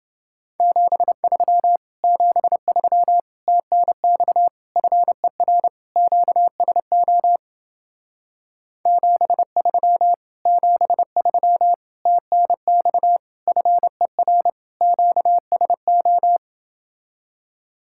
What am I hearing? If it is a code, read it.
Morse code, heard as a keyed tone, about 30 wpm, 700 Hz. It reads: 73 73 TNX FER QSO 73 73 TNX FER QSO